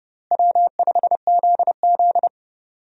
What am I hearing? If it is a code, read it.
Morse code, heard as a keyed tone, about 32 wpm, 700 Hz. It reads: W5ZZ
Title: USA Call Signs - 4 Characters - Repeated 3x - 30wpm